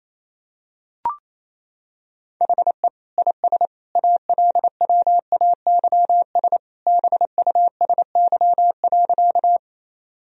Morse code, heard as a keyed tone, about 28 wpm, 700 Hz. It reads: HE IS ALWAYS BUSY.
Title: Sentences from Top 1000 Words - Review Part 3 - 28wpm